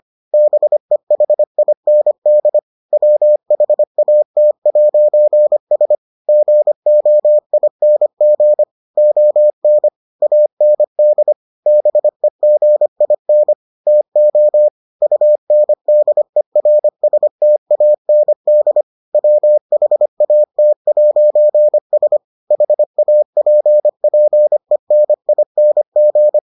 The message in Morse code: BEHIND WHAT'S GOING ON AND BEGIN TO UNDERSTAND WHAT'S HAPPENING

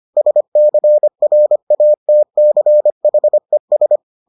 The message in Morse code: SCRATCHES

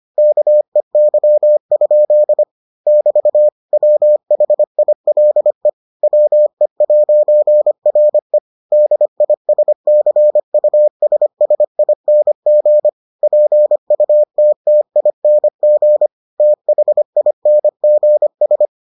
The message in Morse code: KEY? <BT> WHILE WE'RE DISCUSSING PUTTING THINGS

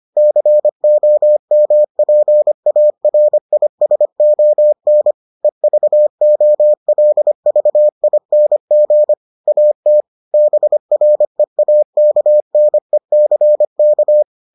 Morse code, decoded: COMPARISON EVOLVING AT BREAKNECK

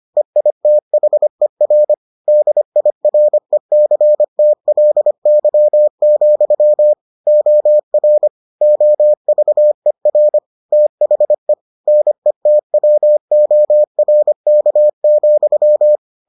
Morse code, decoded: EITHER DIRECTLY, OR OVER THE NETWORK,

